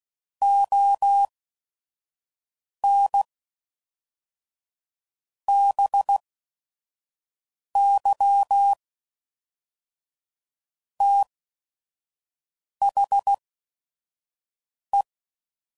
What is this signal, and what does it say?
Morse code, heard as a keyed tone, about 16 wpm, spaced as for 5 wpm, 800 Hz. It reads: ON BY THE
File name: CW Code Groups @5WPM / Technical Subjects / XC